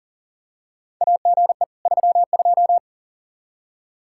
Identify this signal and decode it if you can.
Morse code, heard as a keyed tone, about 37 wpm, 700 Hz. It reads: AGE 32
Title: Practice QSO - Exchange 3 - More Info - 40wpm